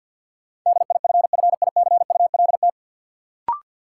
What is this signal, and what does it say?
Morse code, heard as a keyed tone, about 52 wpm, 700 Hz. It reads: DIFFICULT
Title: Sentences from Top 600 Words - New Word Drill - 50wpm